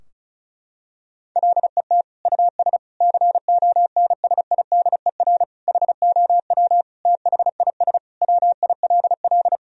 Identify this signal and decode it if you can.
Morse code, heard as a keyed tone, about 34 wpm, 700 Hz. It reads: LET US CONSIDER HOW THIS WILL
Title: Sentences from Top 1000 Words - Review Part 4 - 35wpm